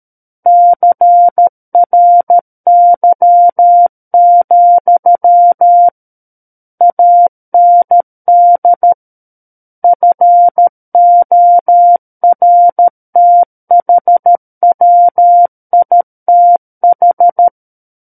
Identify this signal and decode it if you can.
Morse code, heard as a keyed tone, about 13 wpm, 700 Hz. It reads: CRY, AND FORTHWITH
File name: Book One / 04-Chapter 4 - 13 WPM